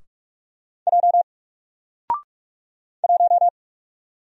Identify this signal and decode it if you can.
Morse code, heard as a keyed tone, about 47 wpm, 700 Hz. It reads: J 1
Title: Single Letter-Number 45wpm